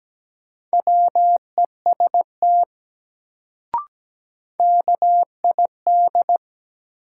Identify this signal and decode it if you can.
Morse code, heard as a keyed tone, about 17 wpm, 700 Hz. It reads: WEST KID
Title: Sets of 5 Words - New Word Drill - 17wpm